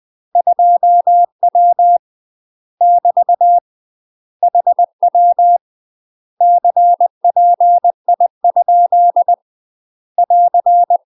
Morse code, heard as a keyed tone, about 20 wpm, 700 Hz. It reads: 2W <BT> HW CPI ? <AR>
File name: UK MASTER QSOs TRAINING/20WPM